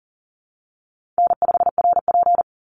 Morse code, heard as a keyed tone, about 40 wpm, 700 Hz. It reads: N5RP
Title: Call Signs Repeated 3x - 40wpm